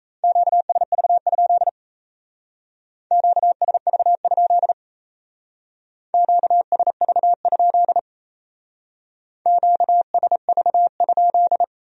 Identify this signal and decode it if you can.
Morse code, heard as a keyed tone, about 34 wpm, 700 Hz. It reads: QSV? QSV? QSV? QSV?
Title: Speed-Racing Q-Codes - 28wpm